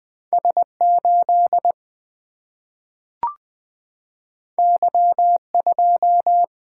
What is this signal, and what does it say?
Morse code, heard as a keyed tone, about 20 wpm, 700 Hz. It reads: S 8 Y 2
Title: Single-Letter Single-Number 20wpm